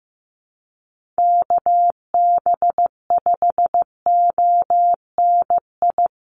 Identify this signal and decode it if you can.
Morse code, heard as a keyed tone, about 15 wpm, 700 Hz. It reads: KB5ONI
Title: USA Call Signs Repeated 3x 15wpm